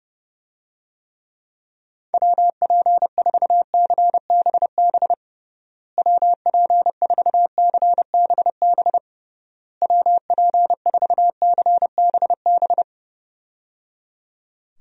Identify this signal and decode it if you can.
Morse code, heard as a keyed tone, about 30 wpm, 700 Hz. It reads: WP4CBB WP4CBB WP4CBB
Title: USA Call Signs - 6 Characters - Repeated 3x - 30wpm